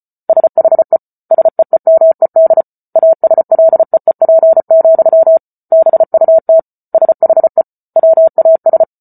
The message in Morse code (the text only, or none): SHE SEEMED ASLEEP, BUT SHE WAS